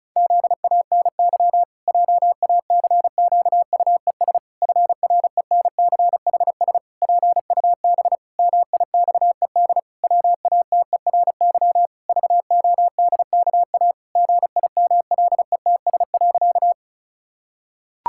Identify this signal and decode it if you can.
Morse code, heard as a keyed tone, about 36 wpm, 700 Hz. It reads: ZANY JACQUES FRENCHS PUB MIXED WATERY VODKA GIMLETS.
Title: Pangram Challenge v2 - 35wpm